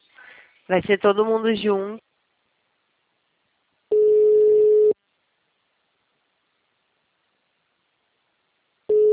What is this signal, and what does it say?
Ringback tone (Europe, 425 Hz), with speech.
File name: out- -2003- - - .109